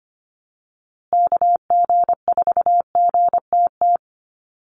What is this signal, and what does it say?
Morse code, heard as a keyed tone, about 25 wpm, 700 Hz. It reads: KG4GTT
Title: USA Call Signs Repeated 2x 25wpm